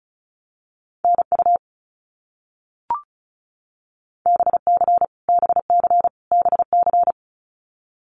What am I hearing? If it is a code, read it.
Morse code, heard as a keyed tone, about 34 wpm, 700 Hz. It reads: NU BC BC BC
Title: Canadian Provinces and Territories - Repeated 3x - 35wpm